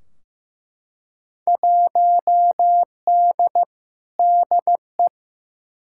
Morse code, heard as a keyed tone, about 15 wpm, 700 Hz. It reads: 1D DE
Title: Field Day Exchanges - 15wpm